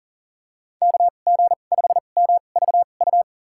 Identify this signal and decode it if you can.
Morse code, heard as a keyed tone, about 37 wpm, spaced as for 28 wpm, 700 Hz. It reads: KG5MVU
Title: USA Call Signs - 6 Characters - 2x Character Spacing - 40wpm